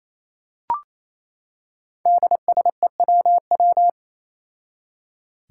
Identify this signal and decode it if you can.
Morse code, heard as a keyed tone, about 28 wpm, 700 Hz. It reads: DSEWW
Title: Five Challenging Letter-Number Combinations 28wpm